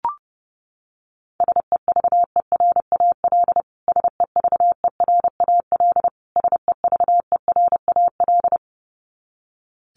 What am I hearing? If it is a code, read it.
Morse code, heard as a keyed tone, about 30 wpm, 700 Hz. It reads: SEVERAL SEVERAL SEVERAL
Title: Top 500 Words Repeated 30wpm